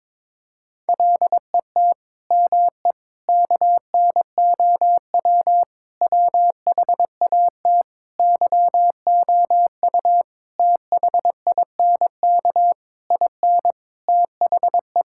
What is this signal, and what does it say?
Morse code, heard as a keyed tone, about 22 wpm, 700 Hz. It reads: LET ME KNOW WHAT YOU THINK IN THE
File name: Sets of 9 Words - 22wpm